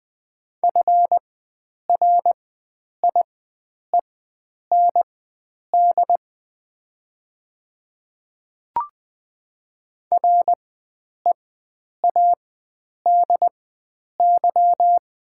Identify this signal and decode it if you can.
Morse code, heard as a keyed tone, about 20 wpm, spaced as for 9 wpm, 700 Hz. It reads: FRIEND READY
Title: 4x Character Spacing - Top 500 Words - 20wpm